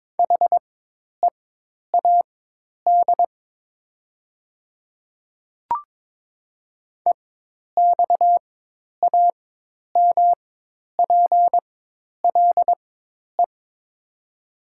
Morse code, heard as a keyed tone, about 22 wpm, spaced as for 10 wpm, 700 Hz. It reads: HEAD EXAMPLE
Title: 4x Character Spacing - Top 500 Words - 22wpm